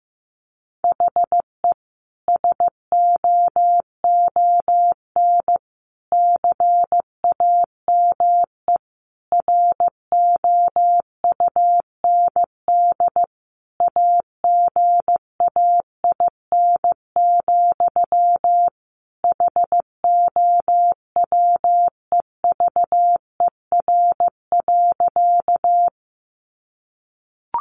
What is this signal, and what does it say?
Morse code, heard as a keyed tone, about 15 wpm, 700 Hz. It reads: HE SOON CAME ROUND AGAIN, HOWEVER.
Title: Sentences from Top 600 Words 15wpm